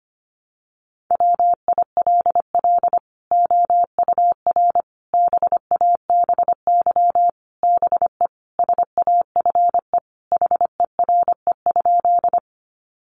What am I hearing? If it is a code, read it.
Morse code, heard as a keyed tone, about 25 wpm, 700 Hz. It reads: WILL OUR BABY BE SAFE HERE?
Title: Sentences from Top 1300 Words 25wpm